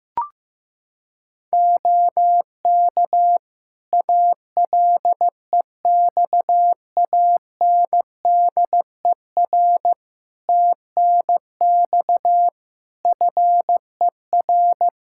Morse code, heard as a keyed tone, about 15 wpm, 700 Hz. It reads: OK ALEXANDER TNX FER
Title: K1USN SST QSO Practice 15wpm